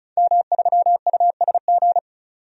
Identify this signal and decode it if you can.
Morse code, heard as a keyed tone, about 34 wpm, 700 Hz. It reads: M3USG
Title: DXCC-100 Call Signs - 5 Characters - 35wpm